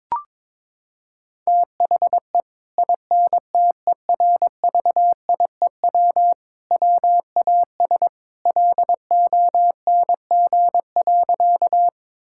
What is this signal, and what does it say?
Morse code, heard as a keyed tone, about 22 wpm, 700 Hz. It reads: THE INTERVIEW WAS LONG.
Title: Sentences from Top 1600 Words 22wpm